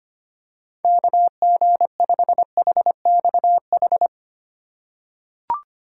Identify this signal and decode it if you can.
Morse code, heard as a keyed tone, about 25 wpm, 700 Hz. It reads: KG5HXH
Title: USA Call Signs - 6 Characters - 4x Character Spacing - 25wpm